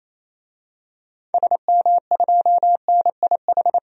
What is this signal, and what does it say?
Morse code, heard as a keyed tone, about 28 wpm, 700 Hz. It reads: SM2NIH
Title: DXCC-100 Call Signs - 6 Characters - 28wpm